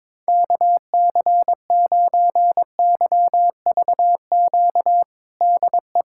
Morse code, heard as a keyed tone, about 22 wpm, 700 Hz. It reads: KC9YVQ DE
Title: USA Callsign DE Callsign - 6 Characters - 22wpm